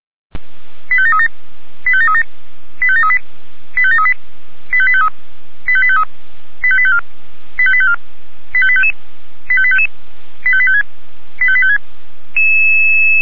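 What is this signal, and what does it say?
Five-tone selective call (ZVEI), heard as quick runs of five tones, tones of about 80 ms. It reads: 85737 85737 85738 85738 85853 85853 85854 85854 85855 85855 85856 85856
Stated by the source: ZVEI five-tone selective call, address 85730, received on an SDR.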